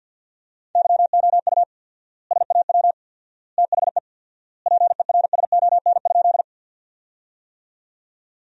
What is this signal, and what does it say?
Morse code, heard as a keyed tone, about 50 wpm, 700 Hz. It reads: YOU SAW THE PERSON?
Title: 4x Word-Spacing - Sentences from Top 400 Words - 50wpm